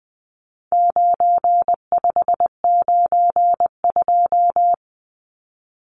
Morse code, heard as a keyed tone, about 20 wpm, 700 Hz. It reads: 9592
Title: Numbers - 4-Digit 20wpm